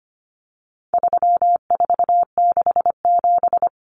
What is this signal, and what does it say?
Morse code, heard as a keyed tone, about 25 wpm, 700 Hz. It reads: 3467
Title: Numbers - 4-Digit 25wpm